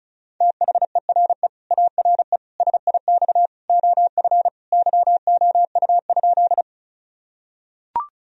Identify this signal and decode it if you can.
Morse code, heard as a keyed tone, about 36 wpm, 700 Hz. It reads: THERE ARE SIX OF YOU?
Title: Sentences from Top 1000 Words - Review Part 2 - 35wpm